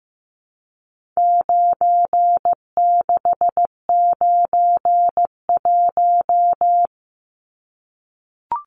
Morse code, heard as a keyed tone, about 15 wpm, 700 Hz. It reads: 9691
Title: Numbers - 4-Digit 15wpm